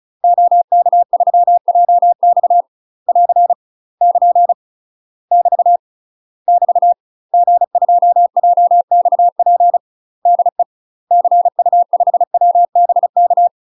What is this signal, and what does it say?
Morse code, heard as a keyed tone, about 36 wpm, 700 Hz. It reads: OK3JX <AR> <KN> <BT> <BT> G2JXP DE CU5WBK
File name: UK MASTER QSOs TRAINING / 35WPM / UK MASTER QSOs 0022